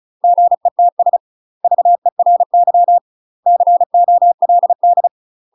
Morse code, heard as a keyed tone, about 34 wpm, 700 Hz. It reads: GETS VERY COLD
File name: UK MASTER QSOs TRAINING/35WPM